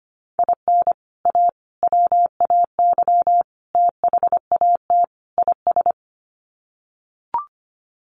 Morse code, heard as a keyed tone, about 25 wpm, 700 Hz. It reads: IN A WAY THAT IS
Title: Sets of 5 Words Encore 25wpm